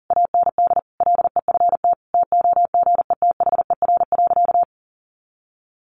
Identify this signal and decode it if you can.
Morse code, heard as a keyed tone, about 40 wpm, 700 Hz. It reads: AND LEFT TOGETHER.
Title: Sentences from Top 400 Words 40wpm